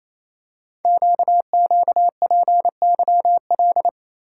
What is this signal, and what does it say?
Morse code, heard as a keyed tone, about 28 wpm, 700 Hz. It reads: QQPYL
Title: Code Groups - 5 Letters - 28wpm